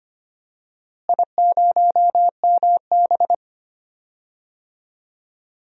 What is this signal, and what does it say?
Morse code, heard as a keyed tone, about 25 wpm, 700 Hz. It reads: I0MB
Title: DXCC-100 Call Signs - 4 Characters - Repeated 3x - 25wpm